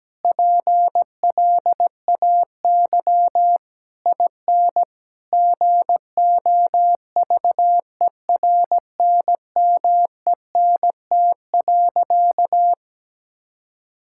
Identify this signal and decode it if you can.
Morse code, heard as a keyed tone, about 17 wpm, 700 Hz. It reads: PLAY IN GOVERNMENT.
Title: Sentences from Top 1000 Words - Review Part 6 - 17wpm